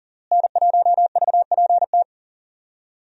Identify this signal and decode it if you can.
Morse code, heard as a keyed tone, about 40 wpm, 700 Hz. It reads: N1VPT
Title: USA Call Signs - 5 Characters - Repeated 3x - 40wpm